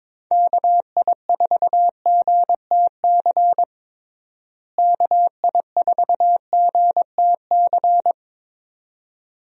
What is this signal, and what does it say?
Morse code, heard as a keyed tone, about 22 wpm, 700 Hz. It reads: KI4GTC KI4GTC
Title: USA Call Signs - 6 Characters - Repeated 2x - 22wpm